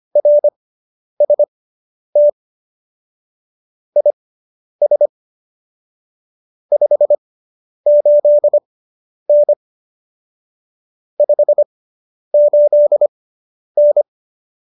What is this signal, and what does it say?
Morse code, heard as a keyed tone, about 25 wpm, spaced as for 10 wpm, 590 Hz. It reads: RST IS 58N 58N